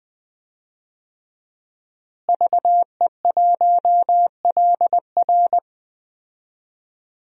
Morse code, heard as a keyed tone, about 20 wpm, 700 Hz. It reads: VE1LR
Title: DXCC-100 Call Signs - 5 Characters - Repeated 2x - 20wpm